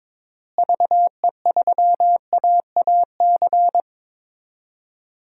Morse code, heard as a keyed tone, about 22 wpm, 700 Hz. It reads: VE3AAC
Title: DXCC-100 Call Signs - 6 Characters - Repeated 3x - 22wpm